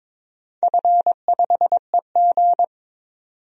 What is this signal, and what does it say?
Morse code, heard as a keyed tone, about 22 wpm, 700 Hz. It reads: F5EG
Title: DXCC-100 Call Signs - 4 Characters - 22wpm